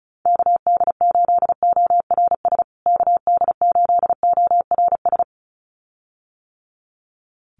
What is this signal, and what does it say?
Morse code, heard as a keyed tone, about 34 wpm, 700 Hz. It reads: KD8ORS KD8ORS
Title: USA Call Signs Repeated 2x 35wpm